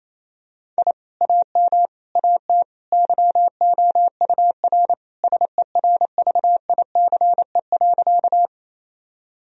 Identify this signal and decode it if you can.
Morse code, heard as a keyed tone, about 27 wpm, 700 Hz. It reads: I AM AT YOUR SERVICE.
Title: Sentences from Top 1000 Words - Review Part 4 - 28wpm